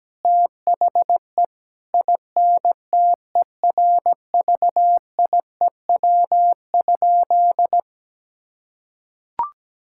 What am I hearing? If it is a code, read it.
Morse code, heard as a keyed tone, about 17 wpm, 700 Hz. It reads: THE INTERVIEW?
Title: Sentences from Top 1600 Words 17wpm